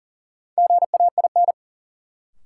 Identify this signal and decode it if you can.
Morse code, heard as a keyed tone, about 40 wpm, 700 Hz. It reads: GAIN
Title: Sentences from Top 1000 Words - New Word Drill - 40wpm